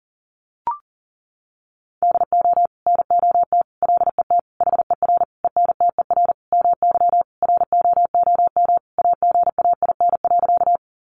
Morse code, heard as a keyed tone, about 40 wpm, 700 Hz. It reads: DO NOT LET HER ENTER MY ROOM AGAIN.
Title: Sentences from Top 1300 Words 40wpm